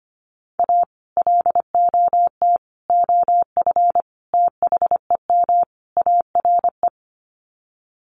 Morse code, heard as a keyed tone, about 25 wpm, 700 Hz. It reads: A LOT OF THEM ARE
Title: Sets of 5 Words Encore2 25wpm